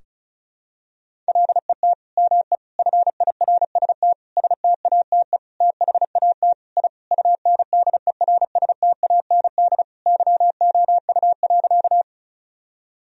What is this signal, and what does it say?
Morse code, heard as a keyed tone, about 34 wpm, 700 Hz. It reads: LET ME FIRST STATE THAT I UNDERSTAND YOU.
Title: Sentences from Top 1000 Words - Review Part 2 - 35wpm